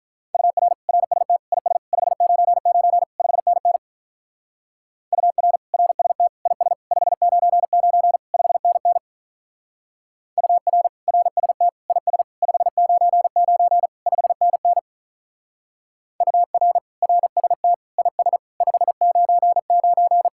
Morse code, heard as a keyed tone, about 45 wpm, 700 Hz. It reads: UR RST IS 599 5NN UR RST IS 599 5NN UR RST IS 599 5NN UR RST IS 599